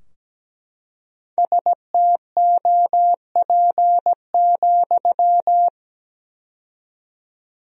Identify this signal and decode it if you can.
Morse code, heard as a keyed tone, about 17 wpm, 700 Hz. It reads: STOP,
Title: 4x Word-Spacing - Sentences from Top 300 Words - 17wpm